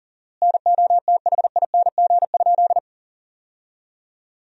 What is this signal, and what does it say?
Morse code, heard as a keyed tone, about 37 wpm, 700 Hz. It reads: NOTHING?
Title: 4x Word-Spacing - Sentences from Top 500 Words - 40wpm